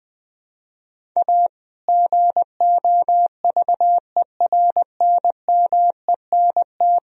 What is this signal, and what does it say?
Morse code, heard as a keyed tone, about 20 wpm, 700 Hz. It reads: A GOVERNMENT
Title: Sentences from Top 1000 Words - Review Part 3 - 20wpm